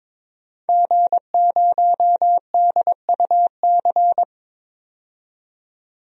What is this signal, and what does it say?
Morse code, heard as a keyed tone, about 22 wpm, 700 Hz. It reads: G0DUC
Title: DXCC-100 Call Signs - 5 Characters - 22wpm